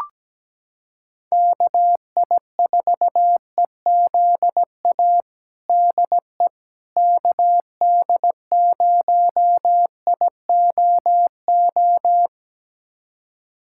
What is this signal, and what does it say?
Morse code, heard as a keyed tone, about 17 wpm, 700 Hz. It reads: KI4EZA DE KD0IOO
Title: USA Callsign DE Callsign - 6 Characters - 17wpm